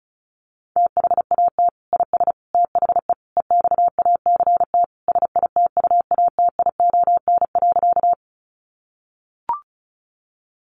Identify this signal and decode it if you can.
Morse code, heard as a keyed tone, about 34 wpm, 700 Hz. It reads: THAT IS THE EXACT SITUATION.
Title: Sentences from Top 1300 Words 35wpm